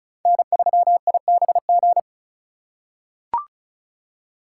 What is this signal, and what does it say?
Morse code, heard as a keyed tone, about 34 wpm, 700 Hz. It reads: N3IBG
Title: USA Call Signs - 5 Characters - 3x Character Spacing - 35wpm